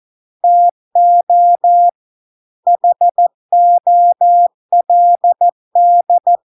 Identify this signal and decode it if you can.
Morse code, heard as a keyed tone, about 14 wpm, 700 Hz. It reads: TO HOLD